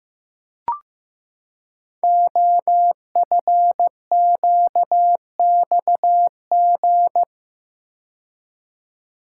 Morse code, heard as a keyed tone, about 15 wpm, 700 Hz. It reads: OFQXG
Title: Code Groups - 5 Letters - 15wpm